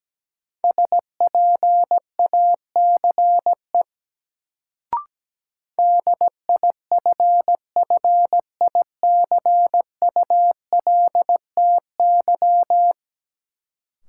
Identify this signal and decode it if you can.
Morse code, heard as a keyed tone, about 17 wpm, 700 Hz. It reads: SPACE DIFFICULTY
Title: Sentences from Top 1500 Words - New Word Drill Alt - 17wpm